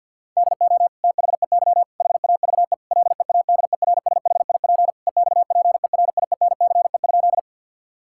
Morse code, heard as a keyed tone, about 51 wpm, 700 Hz. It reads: DO THEY HAVE LEADERSHIP EXPERIENCE?
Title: Sentences from Top 1600 Words 50wpm